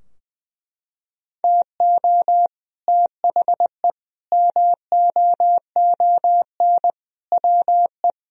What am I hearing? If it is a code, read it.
Morse code, heard as a keyed tone, about 20 wpm, 700 Hz. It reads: TO THE MOON WE
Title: Sentences from Top 1000 Words - Review Part 3 - 20wpm